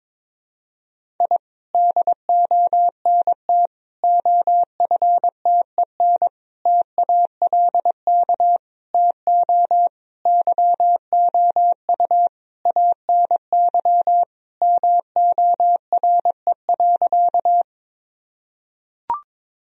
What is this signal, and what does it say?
Morse code, heard as a keyed tone, about 22 wpm, 700 Hz. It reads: I DONT OFTEN TALK TO YOU ANY MORE.
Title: Sentences from Top 1000 Words - Review Part 1 - 22wpm